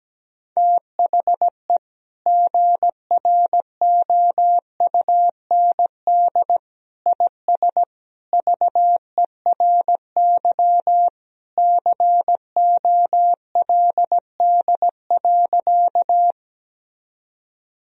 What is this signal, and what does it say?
Morse code, heard as a keyed tone, about 17 wpm, 700 Hz. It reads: THE GROUND IS VERY COLD.
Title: Sentences from Top 1000 Words - Review Part 2 - 17wpm